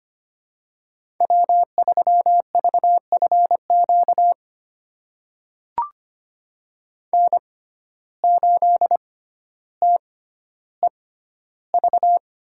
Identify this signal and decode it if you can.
Morse code, heard as a keyed tone, about 25 wpm, spaced as for 13 wpm, 700 Hz. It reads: W3VFQ N8TEV
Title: USA Call Signs - 5 Characters - 6x Character Spacing - 25wpm